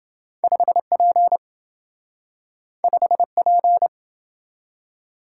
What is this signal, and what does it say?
Morse code, heard as a keyed tone, about 27 wpm, 700 Hz. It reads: 5P 5P